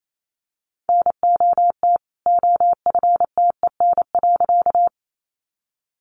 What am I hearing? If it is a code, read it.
Morse code, heard as a keyed tone, about 28 wpm, 700 Hz. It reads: NOT OFTEN.
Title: A Sound Of Thunder 28wpm